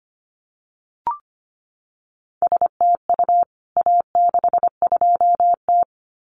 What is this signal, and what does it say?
Morse code, heard as a keyed tone, about 25 wpm, 700 Hz. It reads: STU A62T
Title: CWT Exchange 25wpm